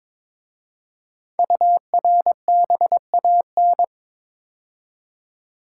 Morse code, heard as a keyed tone, about 22 wpm, 690 Hz. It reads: URBAN
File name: 5 Letter Words - First 3 Letters Given - 22wpm